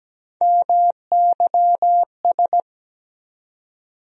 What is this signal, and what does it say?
Morse code, heard as a keyed tone, about 17 wpm, 700 Hz. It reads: MYS
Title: USA Call Sign Postfixes - 3 Letters 17wpm